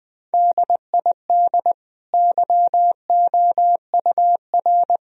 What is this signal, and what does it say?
Morse code, heard as a keyed tone, about 19 wpm, 700 Hz. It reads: DID YOUR
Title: Sentences from Top 1000 Words - Review Part 3 - 20wpm